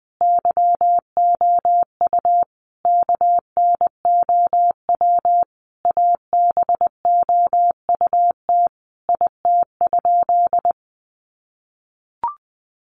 Morse code, heard as a keyed tone, about 20 wpm, 700 Hz. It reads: YOU KNOW ABOUT IT?
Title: Sentences from Top 100 Words 20wpm